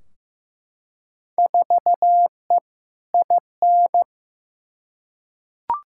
Morse code, heard as a keyed tone, about 15 wpm, 700 Hz. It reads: 4E IN
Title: Field Day Exchanges - 15wpm